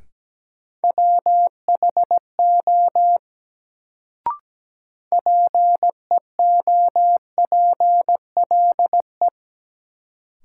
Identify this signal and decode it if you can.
Morse code, heard as a keyed tone, about 17 wpm, 700 Hz. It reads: WHO PEOPLE
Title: Sets of 2 Words - New Word Drill - 17wpm